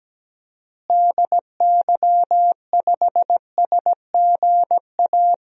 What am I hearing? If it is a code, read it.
Morse code, heard as a keyed tone, about 17 wpm, 700 Hz. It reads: DY5SGA
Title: DXCC-100 Call Signs - 6 Characters - Repeated 2x - 17wpm